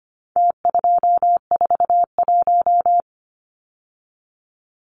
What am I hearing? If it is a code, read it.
Morse code, heard as a keyed tone, about 25 wpm, 700 Hz. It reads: T241
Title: Numbers - Contest Style - 4-Digit 25wpm